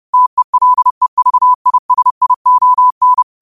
Morse code, heard as a keyed tone, about 30 wpm, 1000 Hz. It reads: TELEVISION
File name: qotdfiles / 30wpm